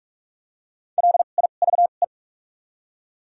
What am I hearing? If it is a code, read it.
Morse code, heard as a keyed tone, about 43 wpm, spaced as for 32 wpm, 700 Hz. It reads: LIVE
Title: 2x Character Spacing - Top 300 Words - 45wpm